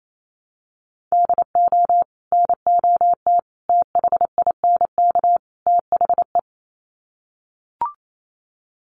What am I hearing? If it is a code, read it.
Morse code, heard as a keyed tone, about 28 wpm, 700 Hz. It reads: DO NOT THINK THE
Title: Sets of 4 Words 28wpm